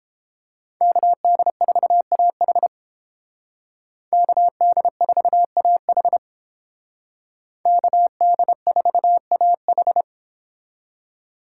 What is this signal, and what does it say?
Morse code, heard as a keyed tone, about 28 wpm, 700 Hz. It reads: KD4AH KD4AH KD4AH